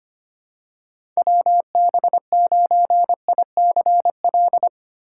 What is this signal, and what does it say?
Morse code, heard as a keyed tone, about 25 wpm, 700 Hz. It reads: WB9ICL
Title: USA Call Signs - 6 Characters - 3x Character Spacing - 25wpm